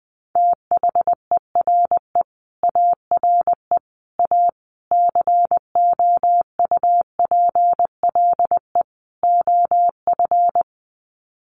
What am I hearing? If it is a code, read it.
Morse code, heard as a keyed tone, about 20 wpm, 700 Hz. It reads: THERE ARE A COUPLE OF
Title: Sets of 5 Words Encore 20wpm